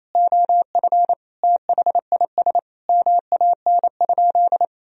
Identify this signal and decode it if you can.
Morse code, heard as a keyed tone, about 27 wpm, 700 Hz. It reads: OF THIS MAN?
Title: Sentences from Top 1000 Words - Review Part 3 - 28wpm